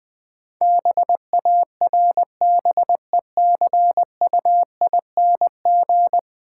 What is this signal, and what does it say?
Morse code, heard as a keyed tone, about 20 wpm, 700 Hz. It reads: BARBECUING
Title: Easy Words - Speed-Racing with Hints - 20wpm